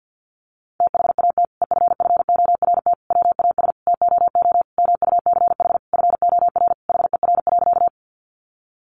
Morse code, heard as a keyed tone, about 50 wpm, 700 Hz. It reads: THAT EFFORT WAS TOO MUCH FOR HER.
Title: Sentences from Top 1200 Words 50wpm